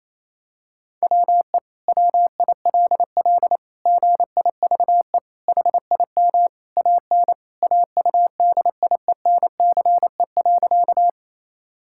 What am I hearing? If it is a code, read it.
Morse code, heard as a keyed tone, about 28 wpm, 700 Hz. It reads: WE WILL GIVE HIM AN AUDIENCE.
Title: Sentences from Top 1000 Words - Review Part 5 - 28wpm